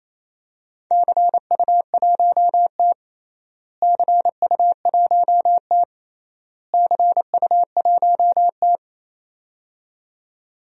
Morse code, heard as a keyed tone, about 28 wpm, 700 Hz. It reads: CU1T CU1T CU1T
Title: DXCC-100 Call Signs - 4 Characters - Repeated 3x - 28wpm